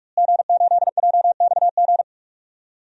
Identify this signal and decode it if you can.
Morse code, heard as a keyed tone, about 43 wpm, 700 Hz. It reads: G8JXG
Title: DXCC-100 Call Signs - 5 Characters - Repeated 2x - 45wpm